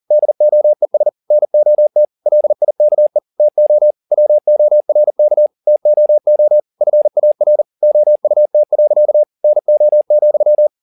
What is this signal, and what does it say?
Morse code, heard as a keyed tone, about 40 wpm, 600 Hz. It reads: DOES NOT LIKE TO WORK TOO FAR OUT. NO,
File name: the old man and the sea 1 40wpm